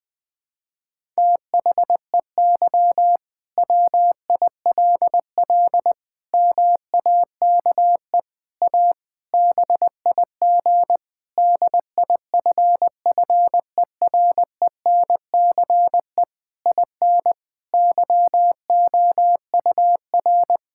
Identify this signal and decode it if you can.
Morse code, heard as a keyed tone, about 21 wpm, 700 Hz. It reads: THEY WILL MAKE A BIG DIFFERENCE IN YOUR
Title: Sentences from Top 1000 Words - Review Part 5 - 20wpm